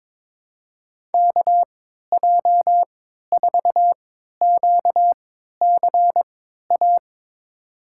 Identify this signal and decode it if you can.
Morse code, heard as a keyed tone, about 22 wpm, spaced as for 12 wpm, 700 Hz. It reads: KJ4QCA
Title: USA Call Signs - 3x Character Spacing - 22wpm